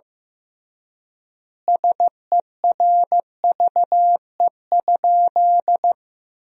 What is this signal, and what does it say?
Morse code, heard as a keyed tone, about 15 wpm, 700 Hz. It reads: SERVE?
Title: 3x Word-Spacing - Sentences from Top 400 Words - 15wpm